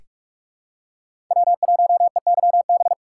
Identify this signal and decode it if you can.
Morse code, heard as a keyed tone, about 46 wpm, 700 Hz. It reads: W1EYB
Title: USA Call Signs - 5 Characters - 6x Character Spacing - 45wpm